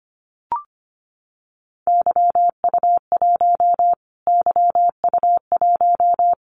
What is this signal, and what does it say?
Morse code, heard as a keyed tone, about 25 wpm, 700 Hz. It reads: YU1 YU1
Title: Callsign Prefixes for Top 25 DXCC - Repeated 3x - 25wpm